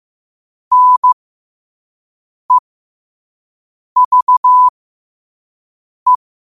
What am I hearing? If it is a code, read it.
Morse code, heard as a keyed tone, about 15 wpm, spaced as for 5 wpm, 1000 Hz. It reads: NEVE
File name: qotdfiles / 5wpm